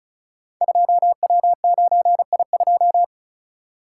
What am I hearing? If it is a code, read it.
Morse code, heard as a keyed tone, about 36 wpm, 700 Hz. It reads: 2W9I2
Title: Code Groups - 5 Letter:Numbers - 35wpm